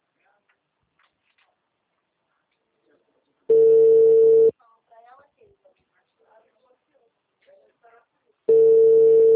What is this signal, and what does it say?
Ringback tone (China, 450 Hz), with speech.